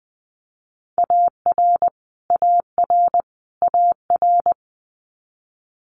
Morse code, heard as a keyed tone, about 20 wpm, 700 Hz. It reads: AR AR AR
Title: Multi-letter Phonemes Repeated 3x 20wpm